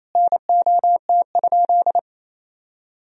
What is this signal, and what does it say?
Morse code, heard as a keyed tone, about 28 wpm, 700 Hz. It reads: NOT?
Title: Sentences from Top 1000 Words - Review Part 4 - 28wpm